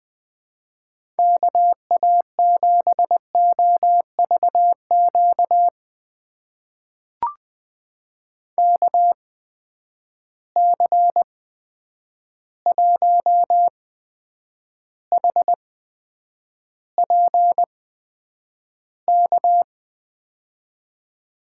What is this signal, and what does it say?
Morse code, heard as a keyed tone, about 20 wpm, spaced as for 9 wpm, 700 Hz. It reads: KA7OVQ KC1HPK
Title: USA Call Signs - 6 Characters - 8x Character Spacing - 20wpm